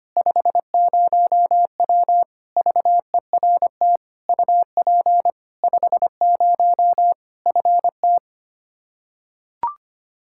Morse code, heard as a keyed tone, about 25 wpm, 700 Hz. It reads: 50W VERT UP 50 FT
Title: Practice QSO - Exchange 2 - Info - 25wpm